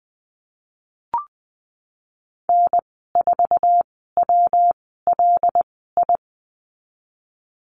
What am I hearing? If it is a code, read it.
Morse code, heard as a keyed tone, about 20 wpm, spaced as for 15 wpm, 700 Hz. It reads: N4WLI
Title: 122 Lesson 20g - Number 3 - Call Signs 20wpm